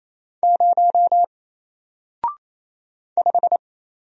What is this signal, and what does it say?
Morse code, heard as a keyed tone, about 28 wpm, 700 Hz. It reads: 0 5
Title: Single Letter-Number Alt 28wpm